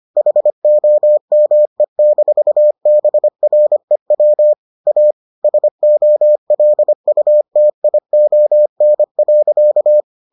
Morse code, heard as a keyed tone, about 25 wpm, 600 Hz. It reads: HOME-BREW A SOLUTION.